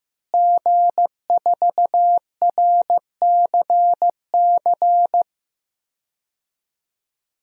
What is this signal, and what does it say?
Morse code, heard as a keyed tone, about 15 wpm, 700 Hz. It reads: G4RCC
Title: DXCC-100 Call Signs - 5 Characters - Repeated 2x - 15wpm